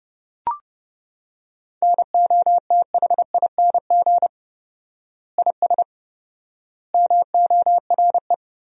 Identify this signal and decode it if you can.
Morse code, heard as a keyed tone, about 31 wpm, 700 Hz. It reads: NOTHING IS MORE
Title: 4x Word-Spacing - Sentences from Top 500 Words - 30wpm